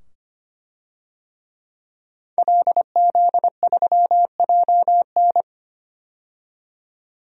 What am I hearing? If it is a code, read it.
Morse code, heard as a keyed tone, about 25 wpm, 700 Hz. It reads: LZ3JN
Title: DXCC-100 Call Signs - 5 Characters - 25wpm